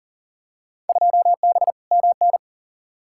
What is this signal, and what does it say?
Morse code, heard as a keyed tone, about 40 wpm, 700 Hz. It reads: 2B MN
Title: Field Day Exchanges - 40wpm